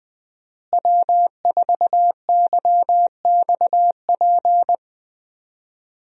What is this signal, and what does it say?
Morse code, heard as a keyed tone, about 20 wpm, 700 Hz. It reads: W4YXP
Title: USA Call Signs - 5 Characters - Repeated 3x - 20wpm